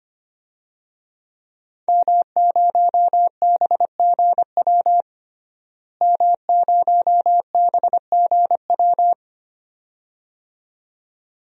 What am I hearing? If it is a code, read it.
Morse code, heard as a keyed tone, about 25 wpm, 700 Hz. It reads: M0BGW M0BGW
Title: DXCC-100 Call Signs - 5 Characters - Repeated 2x - 25wpm